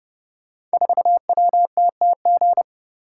Morse code, heard as a keyed tone, about 29 wpm, 700 Hz. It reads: 4WTTG
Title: Code Groups - 5 Letter:Numbers - 30wpm